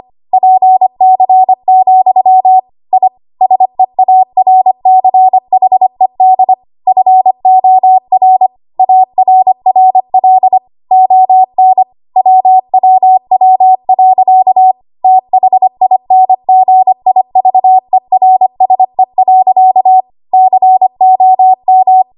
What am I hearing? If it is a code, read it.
Morse code, heard as a keyed tone, about 25 wpm, 750 Hz. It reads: PC, I SEARCHED FOR ARRL ON WWW.THINGIVERSE.COM
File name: Archive / 25 WPM / 25WPM